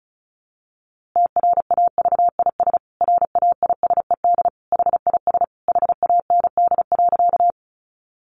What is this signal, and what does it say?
Morse code, heard as a keyed tone, about 34 wpm, 700 Hz. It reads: TRAVIS RAISED HIS HAND.
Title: A Sound Of Thunder 35wpm